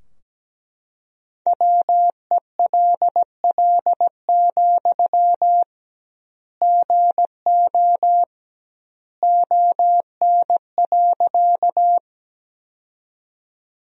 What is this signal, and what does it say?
Morse code, heard as a keyed tone, about 17 wpm, 700 Hz. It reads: WELL, GO ON.
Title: 2x Word-Spacing - Sentences from Top 200 Words - 17wpm